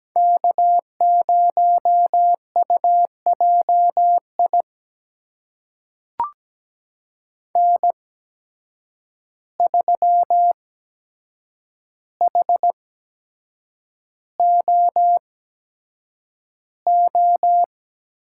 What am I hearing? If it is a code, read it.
Morse code, heard as a keyed tone, about 17 wpm, spaced as for 7 wpm, 700 Hz. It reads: K0UJI N3HOO
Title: USA Call Signs - 5 Characters - 8x Character Spacing - 17wpm